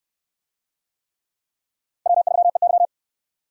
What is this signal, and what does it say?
Morse code, heard as a keyed tone, about 69 wpm, 700 Hz. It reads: W3EY